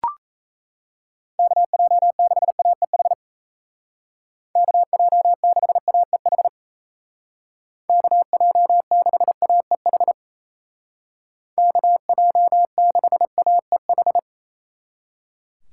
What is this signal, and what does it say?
Morse code, heard as a keyed tone, about 37 wpm, 700 Hz. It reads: KJ6AEH KJ6AEH KJ6AEH KJ6AEH